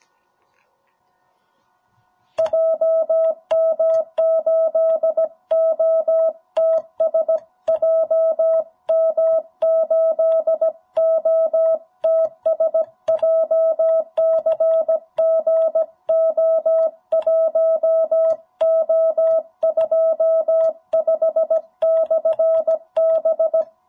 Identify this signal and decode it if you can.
Morse code, heard as a keyed tone, about 17 wpm, 650 Hz. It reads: JM8OTS JM8OTS JCG O1O25/B
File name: jh8cbh / 2021cw